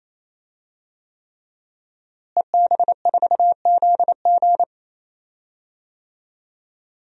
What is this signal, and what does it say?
Morse code, heard as a keyed tone, about 28 wpm, 700 Hz. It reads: EB4ZG
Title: DXCC-100 Call Signs - 5 Characters - Repeated 3x - 28wpm